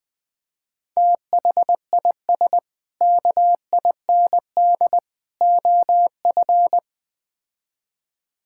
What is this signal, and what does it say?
Morse code, heard as a keyed tone, about 20 wpm, 700 Hz. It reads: THIS KIND OF
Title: Build A Phrase - 3-Word Phrases - 20wpm